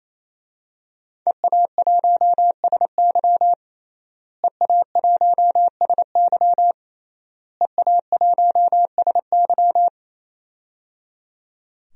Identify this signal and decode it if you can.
Morse code, heard as a keyed tone, about 27 wpm, 700 Hz. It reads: EA1SY EA1SY EA1SY
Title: DXCC-100 Call Signs - 5 Characters - Repeated 3x - 28wpm